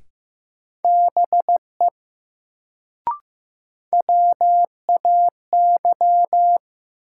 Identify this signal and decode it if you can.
Morse code, heard as a keyed tone, about 15 wpm, 700 Hz. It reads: BE WAY
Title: Sentences from Top 100 Words - New Word Drill - 15wpm